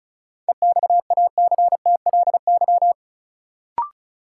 Morse code, heard as a keyed tone, about 34 wpm, 700 Hz. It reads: EXACTLY
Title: Sentences from Top 900 Words - New Word Drill - 35wpm